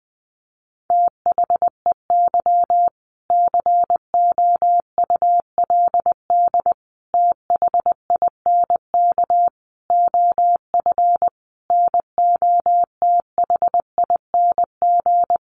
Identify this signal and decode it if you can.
Morse code, heard as a keyed tone, about 20 wpm, 700 Hz. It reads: THEY COULD THINK OF NOTHING
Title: Sentences from Top 800 Words 20wpm